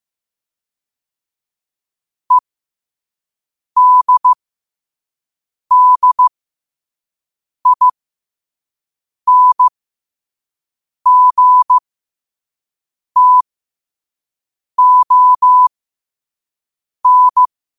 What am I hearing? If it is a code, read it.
Morse code, heard as a keyed tone, about 15 wpm, spaced as for 5 wpm, 1000 Hz. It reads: EDDINGTON
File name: qotdfiles / 5wpm